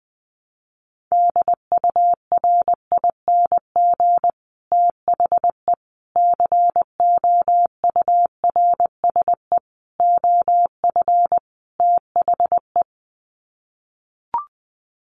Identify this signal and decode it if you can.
Morse code, heard as a keyed tone, about 20 wpm, 700 Hz. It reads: DURING THE COURSE OF THE
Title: Sets of 5 Words Encore 20wpm